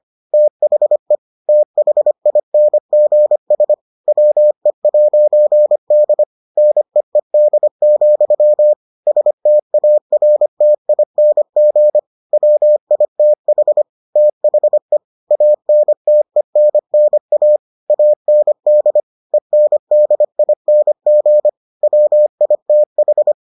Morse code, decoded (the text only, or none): THE THINGS WE'D NEED, STARTING WITH THE ANTENNA AND ENDING WITH